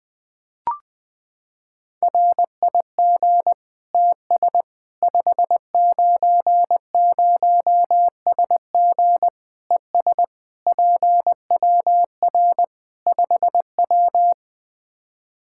Morse code, heard as a keyed tone, about 20 wpm, 700 Hz. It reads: RIG TS 590SG ES PWR 5W
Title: Practice QSO - Exchange 2 - Info - 20wpm